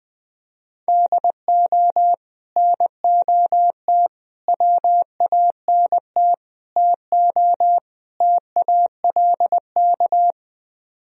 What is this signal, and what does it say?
Morse code, heard as a keyed tone, about 20 wpm, 700 Hz. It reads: DO NOT WANT TO TALK
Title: Speed-Racing - Sets of 5 Words 17wpm